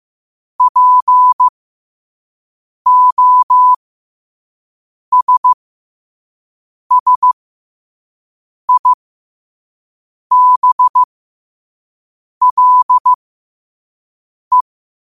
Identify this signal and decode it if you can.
Morse code, heard as a keyed tone, about 15 wpm, spaced as for 5 wpm, 1000 Hz. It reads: POSSIBLE
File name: qotdfiles / 5wpm